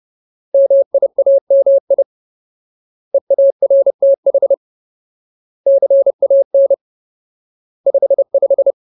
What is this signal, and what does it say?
Morse code, heard as a keyed tone, about 30 wpm, 550 Hz. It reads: MIAMI EARTH CAN 55
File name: Ultimate Word List / 30WPM Ultimate